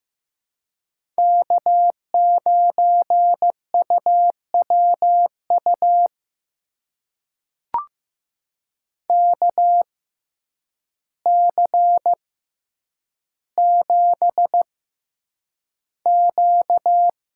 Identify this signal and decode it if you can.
Morse code, heard as a keyed tone, about 15 wpm, spaced as for 8 wpm, 700 Hz. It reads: K9UWU KC7Q
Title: USA Call Signs - 5 Characters - 6x Character Spacing - 15wpm